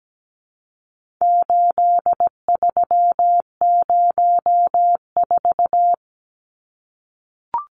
Morse code, heard as a keyed tone, about 17 wpm, 700 Hz. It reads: 8304
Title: Numbers - 4-Digit 17wpm